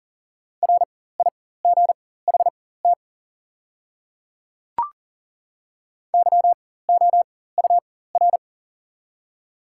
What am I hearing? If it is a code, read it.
Morse code, heard as a keyed tone, about 40 wpm, spaced as for 19 wpm, 700 Hz. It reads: RIGHT YOUR
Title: 4x Character Spacing - Top 200 Words - 40wpm